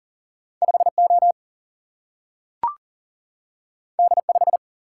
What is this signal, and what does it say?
Morse code, heard as a keyed tone, about 40 wpm, 700 Hz. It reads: 5O D5
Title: Single-Letter Single-Number 40wpm